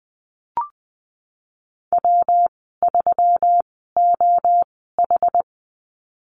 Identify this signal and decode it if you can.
Morse code, heard as a keyed tone, about 20 wpm, spaced as for 15 wpm, 700 Hz. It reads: W3OH
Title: 122 Lesson 20g - Number 3 - Call Signs 20wpm